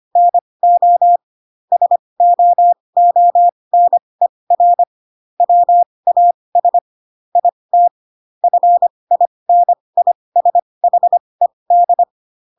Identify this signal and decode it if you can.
Morse code, heard as a keyed tone, about 25 wpm, spaced as for 20 wpm, 700 Hz. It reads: NO SOONER WAS IT FINISHED
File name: Pinocchio / Pinocchio ch3 15 25 ik1hge